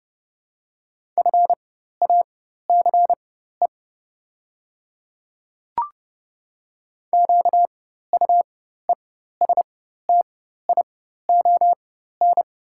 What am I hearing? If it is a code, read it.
Morse code, heard as a keyed tone, about 30 wpm, spaced as for 14 wpm, 690 Hz. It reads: FACE QUESTION
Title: 4x Character Spacing - Top 500 Words - 30wpm